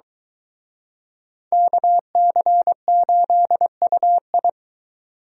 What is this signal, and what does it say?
Morse code, heard as a keyed tone, about 24 wpm, 700 Hz. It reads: KC8UI